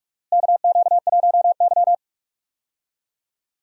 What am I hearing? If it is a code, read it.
Morse code, heard as a keyed tone, about 46 wpm, 700 Hz. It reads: KQ1Y
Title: USA Call Signs - 4 Characters - Repeated 2x - 45wpm